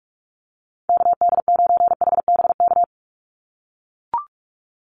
Morse code, heard as a keyed tone, about 44 wpm, 700 Hz. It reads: KD9HBK
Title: USA Call Signs 45wpm